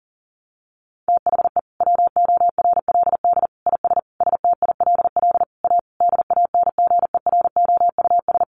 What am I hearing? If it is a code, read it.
Morse code, heard as a keyed tone, about 40 wpm, 700 Hz. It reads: THE WORLD IS STILL A DANGEROUS
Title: Sentences from Top 1200 Words 40wpm